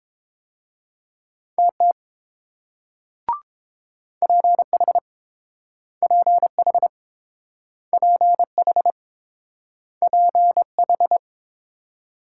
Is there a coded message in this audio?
Yes: Morse code from 1.6 to 11.2 s, heard as a keyed tone, about 30 wpm, 690 Hz. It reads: TT PH PH PH PH